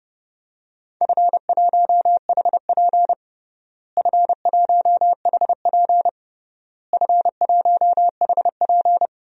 Morse code, heard as a keyed tone, about 32 wpm, 700 Hz. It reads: F1HP F1HP F1HP
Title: DXCC-100 Call Signs - 4 Characters - Repeated 3x - 30wpm